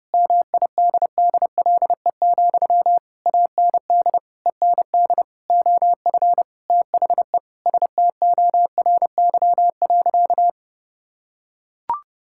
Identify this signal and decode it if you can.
Morse code, heard as a keyed tone, about 32 wpm, 700 Hz. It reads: MIDDLE, AND END OF THE STORY.
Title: Sentences from Top 1000 Words - Review Part 6 - 30wpm